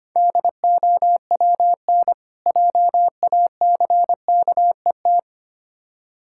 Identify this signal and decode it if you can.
Morse code, heard as a keyed tone, about 26 wpm, 700 Hz. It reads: DOWN JACKET
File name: practice / 25wpm / Difficult Words - Speed-Racing with Hints - 25wpm